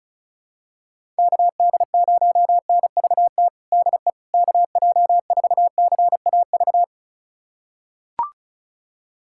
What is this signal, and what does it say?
Morse code, heard as a keyed tone, about 36 wpm, 700 Hz. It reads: KD0NVT DE KJ4CAV
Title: USA Callsign DE Callsign 35wpm